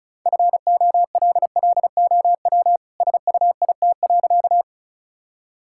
Morse code, heard as a keyed tone, about 34 wpm, 700 Hz. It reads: FOLLOW SUIT.
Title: Sentences from Top 1600 Words 35wpm